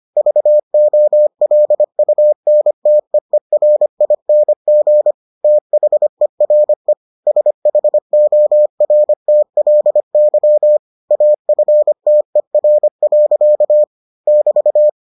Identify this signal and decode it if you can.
Morse code, heard as a keyed tone, about 25 wpm, 600 Hz. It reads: VOLUNTEERING THERE SHORTLY AFTER. <BT>